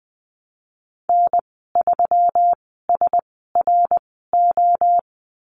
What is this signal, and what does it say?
Morse code, heard as a keyed tone, about 20 wpm, spaced as for 15 wpm, 700 Hz. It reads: N3SRO
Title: 140 Lesson 23g - Letter f - Call Signs 20wpm